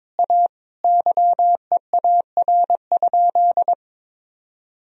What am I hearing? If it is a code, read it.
Morse code, heard as a keyed tone, about 21 wpm, 700 Hz. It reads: A YEAR?
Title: Sentences from Top 1000 Words - Review Part 2 - 22wpm